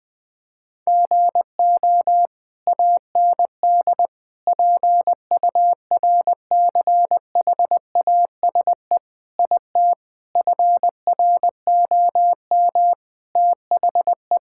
Morse code, heard as a keyed tone, about 21 wpm, 700 Hz. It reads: GO AND PURCHASE IT FROM THE
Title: Sentences from Top 1000 Words - Review Part 5 - 20wpm